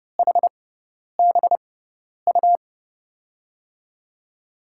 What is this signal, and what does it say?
Morse code, heard as a keyed tone, about 31 wpm, spaced as for 10 wpm, 700 Hz. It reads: HBU
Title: USA Call Signs - 6x Character Spacing - 30wpm